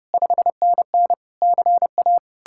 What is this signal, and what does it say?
Morse code, heard as a keyed tone, about 36 wpm, 700 Hz. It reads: 5NN CA
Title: Speed-Racing Common QSO Phrases - 30wpm